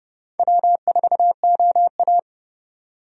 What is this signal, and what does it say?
Morse code, heard as a keyed tone, about 31 wpm, 700 Hz. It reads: W4OA
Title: USA Call Signs - 4 Characters - 30wpm